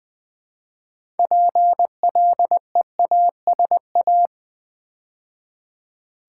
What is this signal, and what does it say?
Morse code, heard as a keyed tone, about 20 wpm, 700 Hz. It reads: PLEASA
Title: Build A Word - 3 Letters At A Time - 8-Letter Words - 20wpm